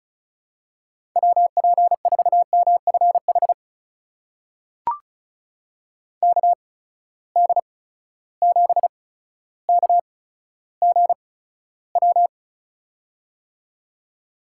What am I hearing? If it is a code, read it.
Morse code, heard as a keyed tone, about 34 wpm, spaced as for 15 wpm, 700 Hz. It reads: WP4MFH KD7KGW
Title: USA Call Signs - 8x Character Spacing - 35wpm